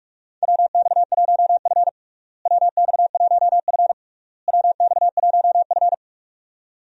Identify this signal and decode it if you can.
Morse code, heard as a keyed tone, about 46 wpm, 700 Hz. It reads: WX1F WX1F WX1F
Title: USA Call Signs - 4 Characters - Repeated 3x - 45wpm